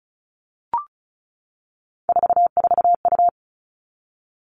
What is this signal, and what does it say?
Morse code, heard as a keyed tone, about 34 wpm, 700 Hz. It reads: 44U
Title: All Cut Numbers - 3-Digit 35wpm